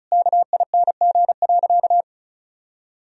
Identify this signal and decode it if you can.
Morse code, heard as a keyed tone, about 36 wpm, 700 Hz. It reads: KING.
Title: Sentences from Top 1000 Words - Review Part 4 - 35wpm